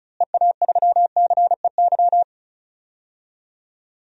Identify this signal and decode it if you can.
Morse code, heard as a keyed tone, about 36 wpm, 700 Hz. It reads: EA3CEY
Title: DXCC-100 Call Signs - 6 Characters - Repeated 2x - 35wpm